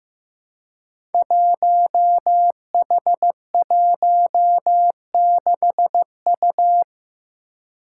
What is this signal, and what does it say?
Morse code, heard as a keyed tone, about 16 wpm, 700 Hz. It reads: 1H16U
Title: Five Challenging Letter-Number Combinations 15wpm